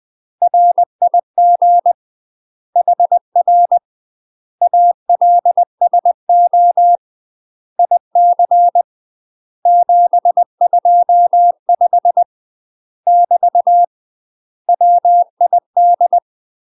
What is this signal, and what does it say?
Morse code, heard as a keyed tone, about 20 wpm, 700 Hz. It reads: RIG HR ALSO IC 725 <BT> WID
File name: UK MASTER QSOs TRAINING / 20WPM / UK MASTER QSOs 0068